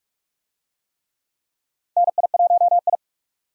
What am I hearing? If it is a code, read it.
Morse code, heard as a keyed tone, about 46 wpm, 700 Hz. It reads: NI1I
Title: USA Call Signs - 4 Characters - 45wpm